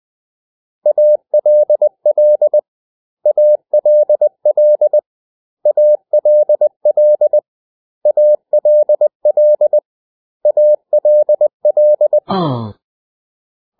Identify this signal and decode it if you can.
Morse code, heard as a keyed tone, about 20 wpm, 600 Hz. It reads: ALL ALL ALL ALL ALL
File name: cw / 100 Common English Words